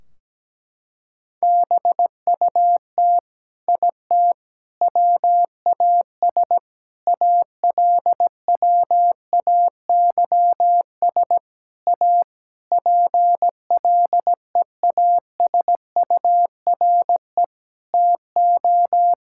Morse code, heard as a keyed tone, about 17 wpm, 700 Hz. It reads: BUT IT WAS ALWAYS A PLEASURE TO
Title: Sentences from Top 1600 Words 17wpm